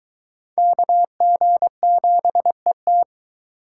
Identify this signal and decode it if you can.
Morse code, heard as a keyed tone, about 22 wpm, 700 Hz. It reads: KG7ET